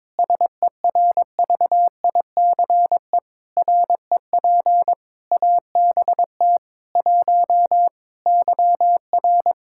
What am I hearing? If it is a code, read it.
Morse code, heard as a keyed tone, about 21 wpm, 700 Hz. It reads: SERVICE REP ABT 1 YR
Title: Practice QSO - Exchange 3 - More Info - 22wpm